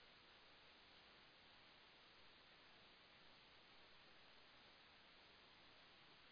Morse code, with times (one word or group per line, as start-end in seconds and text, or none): none